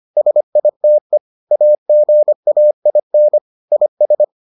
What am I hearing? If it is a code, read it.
Morse code, heard as a keyed tone, about 25 wpm, 600 Hz. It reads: SITE AGAIN IS